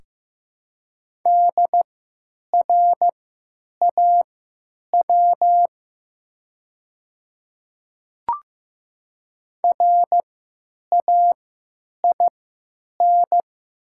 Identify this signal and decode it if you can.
Morse code, heard as a keyed tone, about 15 wpm, spaced as for 9 wpm, 700 Hz. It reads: DRAW RAIN
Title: 3x Character Spacing - Top 300 Words - 15wpm